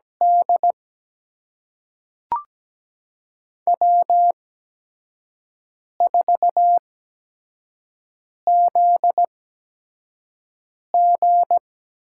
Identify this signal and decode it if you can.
Morse code, heard as a keyed tone, about 17 wpm, spaced as for 5 wpm, 700 Hz. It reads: D W4ZG
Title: USA Call Signs - 4 Characters - 8x Character Spacing - 17wpm